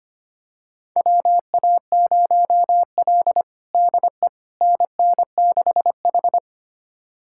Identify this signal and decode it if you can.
Morse code, heard as a keyed tone, about 25 wpm, 700 Hz. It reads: WA0L DE NN6H
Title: USA Callsign DE Callsign - 4 Characters - 25wpm